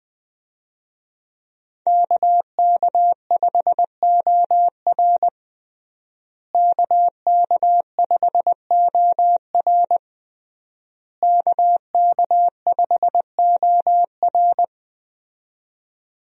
Morse code, heard as a keyed tone, about 21 wpm, 700 Hz. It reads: KK5OR KK5OR KK5OR
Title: USA Call Signs - 5 Characters - Repeated 3x - 20wpm